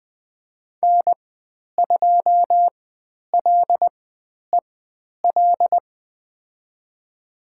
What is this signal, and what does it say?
Morse code, heard as a keyed tone, about 20 wpm, spaced as for 10 wpm, 700 Hz. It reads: N2LEL
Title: 095 Lesson 16g - Letter c - Call Signs 20wpm